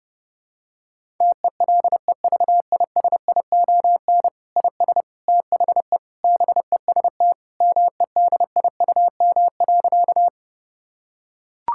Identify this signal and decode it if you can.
Morse code, heard as a keyed tone, about 32 wpm, 700 Hz. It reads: TELEVISION IS THE BEST MEDIUM.
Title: Sentences from Top 1000 Words - Review Part 6 - 30wpm